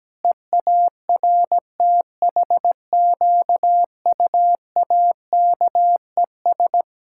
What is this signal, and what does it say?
Morse code, heard as a keyed tone, about 17 wpm, 700 Hz. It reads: EARTHQUAKES
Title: Pangram Challenge v2 - 17wpm